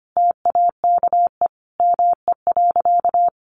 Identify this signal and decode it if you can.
Morse code, heard as a keyed tone, about 25 wpm, 700 Hz. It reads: TAKE ME.
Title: Sentences from Top 400 Words 25wpm